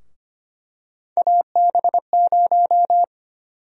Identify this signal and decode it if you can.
Morse code, heard as a keyed tone, about 25 wpm, 700 Hz. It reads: AB0
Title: USA Call Sign Prefixes - Repeated 2x - 25wpm